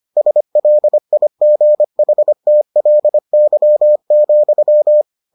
Morse code, heard as a keyed tone, about 25 wpm, 600 Hz. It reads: SLIGHTLY,